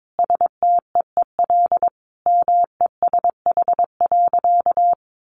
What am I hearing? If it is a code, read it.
Morse code, heard as a keyed tone, about 22 wpm, 700 Hz. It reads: STEEL MESH.
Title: A Sound Of Thunder 22wpm